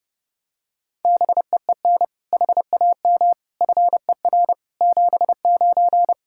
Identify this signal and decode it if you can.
Morse code, heard as a keyed tone, about 29 wpm, 700 Hz. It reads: BEEN HAM FER 79
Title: Practice QSO - Exchange 3 - More Info - 30wpm